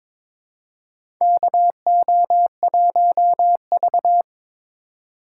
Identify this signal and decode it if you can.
Morse code, heard as a keyed tone, about 22 wpm, 690 Hz. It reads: KO1V
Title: USA Call Signs - 4 Characters - 3x Character Spacing - 22wpm